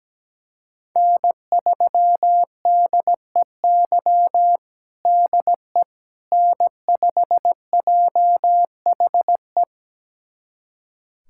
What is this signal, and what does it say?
Morse code, heard as a keyed tone, about 17 wpm, 700 Hz. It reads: N3DEY DE N5JHE
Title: USA Callsign DE Callsign - 5 Characters - 17wpm